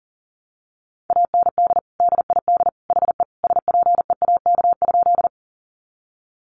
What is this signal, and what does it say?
Morse code, heard as a keyed tone, about 40 wpm, 700 Hz. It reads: AND DID HE SPEAK?
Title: Sentences from Top 1100 Words 40wpm